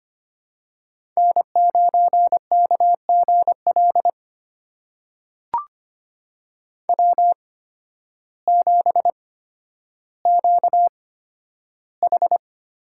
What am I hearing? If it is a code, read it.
Morse code, heard as a keyed tone, about 25 wpm, spaced as for 12 wpm, 700 Hz. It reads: N9KGL W7QH
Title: USA Call Signs - 8x Character Spacing - 25wpm